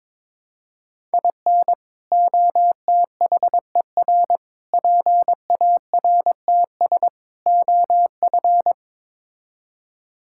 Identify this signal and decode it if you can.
Morse code, heard as a keyed tone, about 22 wpm, 700 Hz. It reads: IN OTHER PARTS OF
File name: practice / 22wpm / Build A Phrase - 5-Word Phrases - 22wpm